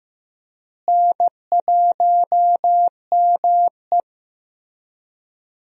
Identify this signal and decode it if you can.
Morse code, heard as a keyed tone, about 15 wpm, 700 Hz. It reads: N1ME
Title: USA Call Signs - 4 Characters - Repeated 3x - 15wpm